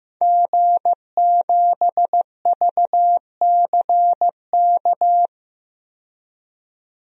Morse code, heard as a keyed tone, about 15 wpm, 700 Hz. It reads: G7VCK
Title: DXCC-100 Call Signs - 5 Characters - 15wpm